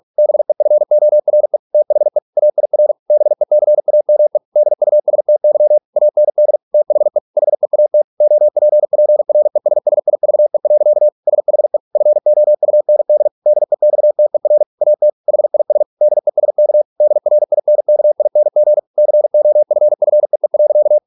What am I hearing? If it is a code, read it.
Morse code, heard as a keyed tone, about 46 wpm, 600 Hz. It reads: BEFORE THE AIR BECAME DUSTY AND THE HEAT OPPRESSIVE. SHE FOUND DEXTER AT HIS DESK DRINKING COFFEE.